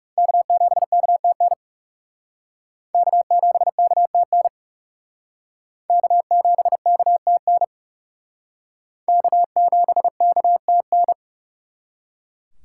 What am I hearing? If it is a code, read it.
Morse code, heard as a keyed tone, about 38 wpm, 700 Hz. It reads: K7KTN K7KTN K7KTN K7KTN